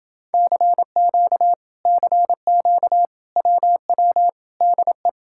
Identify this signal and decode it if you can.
Morse code, heard as a keyed tone, about 29 wpm, 700 Hz. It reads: CQ CQ WW DE